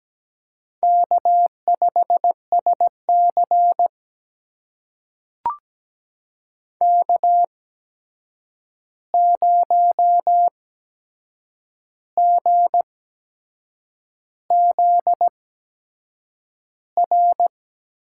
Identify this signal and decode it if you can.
Morse code, heard as a keyed tone, about 17 wpm, spaced as for 7 wpm, 700 Hz. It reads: K5SC K0GZR
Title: USA Call Signs - 8x Character Spacing - 17wpm